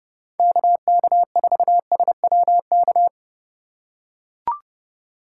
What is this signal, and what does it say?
Morse code, heard as a keyed tone, about 30 wpm, 700 Hz. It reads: KK4SWK